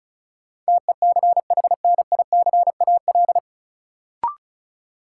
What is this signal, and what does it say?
Morse code, heard as a keyed tone, about 34 wpm, 700 Hz. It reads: TECHNICAL
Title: Sentences from Top 1100 Words - New Word Drill - 35wpm